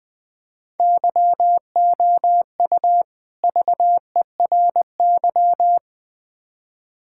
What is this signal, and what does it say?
Morse code, heard as a keyed tone, about 21 wpm, 700 Hz. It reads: YOU VERY
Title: Build A Phrase - 3-Word Phrases - 20wpm